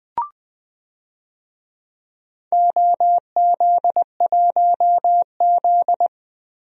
Morse code, heard as a keyed tone, about 20 wpm, 700 Hz. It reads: OZ1Z
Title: DXCC-100 Call Signs - 4 Characters - Repeated 2x - 20wpm